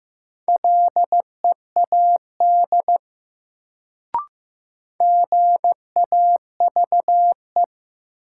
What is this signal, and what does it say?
Morse code, heard as a keyed tone, about 15 wpm, 700 Hz. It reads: LEAD GAVE
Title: Sentences from Top 500 Words - New Word Drill - 15wpm